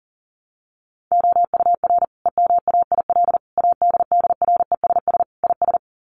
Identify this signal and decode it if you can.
Morse code, heard as a keyed tone, about 40 wpm, 700 Hz. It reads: OUR EMAIL ADDRESS IS
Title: Sets of 5 Words Encore2 40wpm